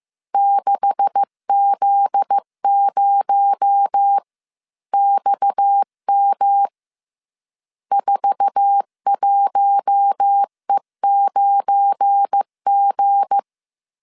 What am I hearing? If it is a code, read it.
Morse code, heard as a keyed tone, about 15 wpm, 800 Hz. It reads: 6Z0XM 41E9G
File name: CW / DET / 1 9 1500C S C 14wpm 3245